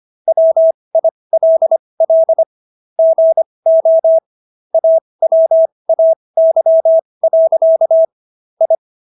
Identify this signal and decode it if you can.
Morse code, heard as a keyed tone, about 25 wpm, spaced as for 20 wpm, 650 Hz. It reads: WILL GO AWAY. I